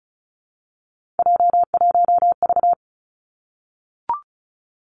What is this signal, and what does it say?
Morse code, heard as a keyed tone, about 36 wpm, 700 Hz. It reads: J1V
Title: Three Challenging Letter-Number Combinations 35wpm